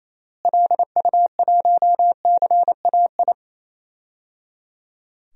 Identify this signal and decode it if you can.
Morse code, heard as a keyed tone, about 27 wpm, 700 Hz. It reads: LU1CAI
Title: DXCC-100 Call Signs - 6 Characters - Repeated 3x - 28wpm